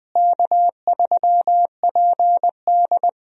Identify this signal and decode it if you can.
Morse code, heard as a keyed tone, about 20 wpm, 700 Hz. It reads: K3PD
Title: USA Call Signs - 4 Characters - Repeated 3x - 20wpm